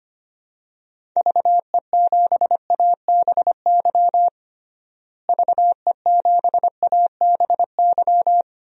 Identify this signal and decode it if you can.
Morse code, heard as a keyed tone, about 25 wpm, 700 Hz. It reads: VE7ABY VE7ABY
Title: DXCC-100 Call Signs - 6 Characters - Repeated 2x - 25wpm